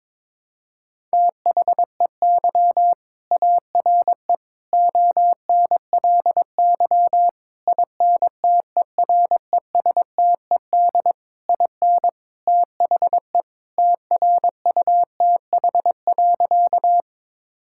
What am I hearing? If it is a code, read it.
Morse code, heard as a keyed tone, about 22 wpm, 700 Hz. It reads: THEY ARE ONLY INTERESTED IN THE TRUTH.
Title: Sentences from Top 1000 Words - Review Part 6 - 22wpm